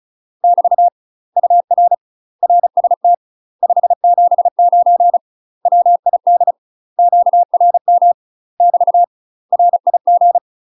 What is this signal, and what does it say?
Morse code, heard as a keyed tone, about 34 wpm, 700 Hz. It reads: <BT> UR RST 579 WID QRM <BT> RIG
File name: UK MASTER QSOs TRAINING / 35WPM / UK MASTER QSOs 0035